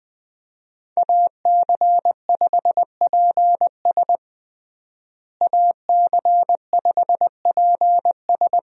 Morse code, heard as a keyed tone, about 20 wpm, 700 Hz. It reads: AC5PS AC5PS
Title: USA Call Signs - 5 Characters - Repeated 3x - 20wpm